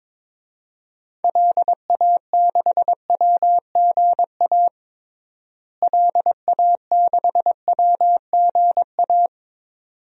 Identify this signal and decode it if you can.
Morse code, heard as a keyed tone, about 21 wpm, 700 Hz. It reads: LA6WGA LA6WGA
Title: DXCC-100 Call Signs - 6 Characters - Repeated 2x - 22wpm